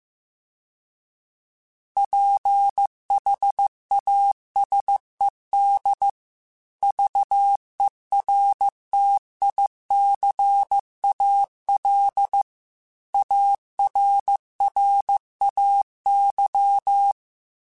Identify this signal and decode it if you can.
Morse code, heard as a keyed tone, about 15 wpm, 800 Hz. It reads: PHASED VERTICAL ARRAY